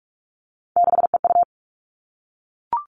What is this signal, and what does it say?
Morse code, heard as a keyed tone, about 44 wpm, 700 Hz. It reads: 6EU
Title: All Cut Numbers - 3-Digit 45wpm